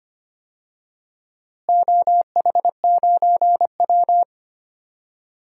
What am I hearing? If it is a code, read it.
Morse code, heard as a keyed tone, about 25 wpm, 700 Hz. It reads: OH9W
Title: DXCC-100 Call Signs - 4 Characters - Repeated 2x - 25wpm